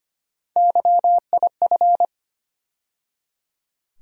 Morse code, heard as a keyed tone, about 25 wpm, 700 Hz. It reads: YIF
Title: USA Call Sign Postfixes - 3 Letters 25wpm